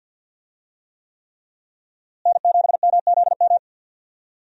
Morse code, heard as a keyed tone, about 51 wpm, 700 Hz. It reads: N7MZM
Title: USA Call Signs - 5 Characters - Repeated 3x - 50wpm